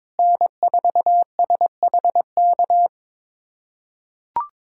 Morse code, heard as a keyed tone, about 22 wpm, 700 Hz. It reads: N4SHK
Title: USA Callsign DE Callsign 22wpm